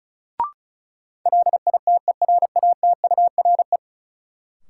Morse code, heard as a keyed tone, about 36 wpm, 700 Hz. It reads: LITERATURE
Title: Sentences from Top 600 Words - New Word Drill - 35wpm